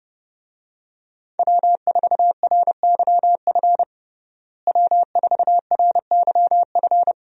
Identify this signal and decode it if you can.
Morse code, heard as a keyed tone, about 30 wpm, 700 Hz. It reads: W4RYF W4RYF
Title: USA Call Signs - 5 Characters - Repeated 2x - 30wpm